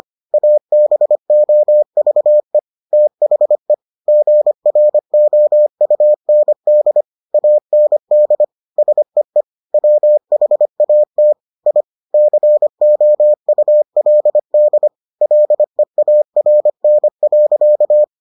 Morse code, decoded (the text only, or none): ABOVE THE GROUND AND SEE WHAT I COULD LEARN.